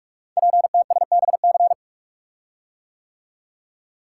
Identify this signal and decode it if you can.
Morse code, heard as a keyed tone, about 44 wpm, 700 Hz. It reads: PTSBC
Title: Code Groups - 5 Letters - 45wpm